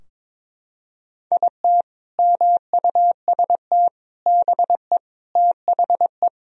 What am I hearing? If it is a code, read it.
Morse code, heard as a keyed tone, about 22 wpm, 700 Hz. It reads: IT MUST BE THE
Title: Sentences from Top 1000 Words - Review Part 4 - 22wpm